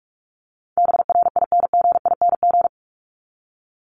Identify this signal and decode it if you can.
Morse code, heard as a keyed tone, about 44 wpm, 700 Hz. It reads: BRINGING
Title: ING Suffix - Top 100 Words with Base Words 45wpm